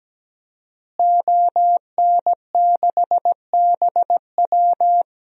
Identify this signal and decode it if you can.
Morse code, heard as a keyed tone, about 17 wpm, 700 Hz. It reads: ON6BW
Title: DXCC-100 Call Signs - 5 Characters - Repeated 2x - 17wpm